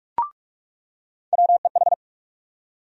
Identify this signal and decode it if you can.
Morse code, heard as a keyed tone, about 46 wpm, 700 Hz. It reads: WEH
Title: USA Call Sign Postfixes - 3 Letters 45wpm